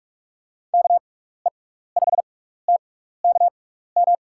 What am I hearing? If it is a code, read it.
Morse code, heard as a keyed tone, about 44 wpm, spaced as for 15 wpm, 700 Hz. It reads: KE5TKM
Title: USA Call Signs - 6 Characters - 6x Character Spacing - 45wpm